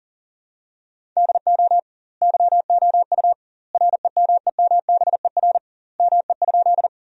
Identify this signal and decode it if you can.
Morse code, heard as a keyed tone, about 40 wpm, 700 Hz. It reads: DO YOU REMEMBER ME?
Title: 2x Word-Spacing - Sentences from Top 400 Words - 40wpm